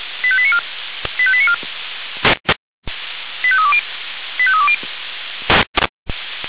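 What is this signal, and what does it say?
Five-tone selective call (ZVEI), heard as quick runs of five tones, tones of about 70 ms. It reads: 85894 85894 85320 85320
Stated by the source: ZVEI five-tone selective call 85320 received on SDR.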